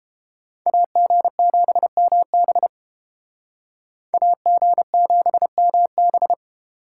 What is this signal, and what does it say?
Morse code, heard as a keyed tone, about 29 wpm, 700 Hz. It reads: AG7MB AG7MB